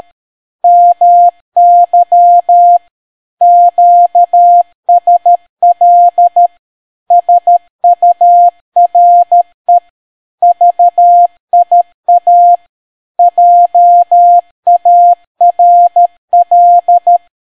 Morse code, heard as a keyed tone, about 13 wpm, 700 Hz. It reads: MY QSL SURE VIA JARL